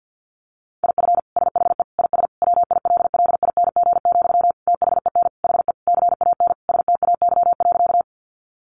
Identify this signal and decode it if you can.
Morse code, heard as a keyed tone, about 49 wpm, 700 Hz. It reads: IF SHE IS WILLING, THEN HE CAN STAY.
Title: Sentences from Top 1000 Words 50wpm